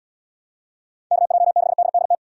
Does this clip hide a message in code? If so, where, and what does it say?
Yes: Morse code from 1.1 to 2.1 s, heard as a keyed tone, about 68 wpm, 700 Hz. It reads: KJ6DDT